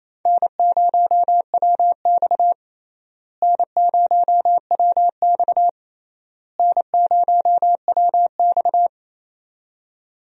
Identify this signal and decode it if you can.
Morse code, heard as a keyed tone, about 28 wpm, 700 Hz. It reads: N0WX N0WX N0WX
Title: USA Call Signs - 4 Characters - Repeated 3x - 28wpm